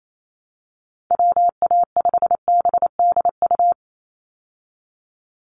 Morse code, heard as a keyed tone, about 28 wpm, 700 Hz. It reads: WA5BDU
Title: Call Signs - 28wpm